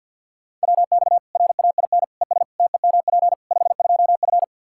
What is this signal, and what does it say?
Morse code, heard as a keyed tone, about 52 wpm, 700 Hz. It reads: WX RAIN ES TEMP 52F
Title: Practice QSO - Exchange 2 - Info - 50wpm